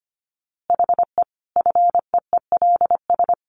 Morse code, heard as a keyed tone, about 25 wpm, 700 Hz. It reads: HE FEELS
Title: Easy Phrases 25wpm